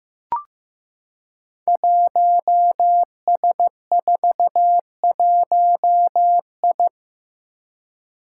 Morse code, heard as a keyed tone, about 15 wpm, 700 Hz. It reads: 1S41I
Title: Five Challenging Letter-Number Combinations 15wpm